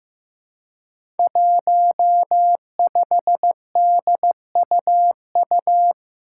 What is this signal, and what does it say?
Morse code, heard as a keyed tone, about 15 wpm, 700 Hz. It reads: 15DUU
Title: Five Challenging Letter-Number Combinations 15wpm